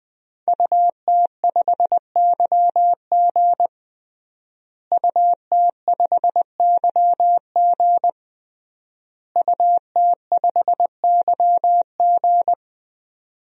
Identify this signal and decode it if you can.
Morse code, heard as a keyed tone, about 20 wpm, 700 Hz. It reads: UT5YG UT5YG UT5YG
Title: DXCC-100 Call Signs - 5 Characters - Repeated 3x - 20wpm